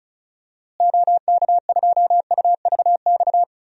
Morse code, heard as a keyed tone, about 36 wpm, 700 Hz. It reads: OK2UVX
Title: DXCC-100 Call Signs - 6 Characters - Repeated 3x - 35wpm